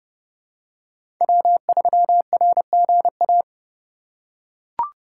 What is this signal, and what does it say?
Morse code, heard as a keyed tone, about 32 wpm, 700 Hz. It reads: W3RGA
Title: K1USN SST QSO Practice - 30wpm